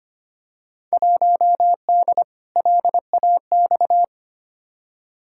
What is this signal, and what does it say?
Morse code, heard as a keyed tone, about 25 wpm, 700 Hz. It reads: 1D LAX
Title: Field Day Exchanges - 25wpm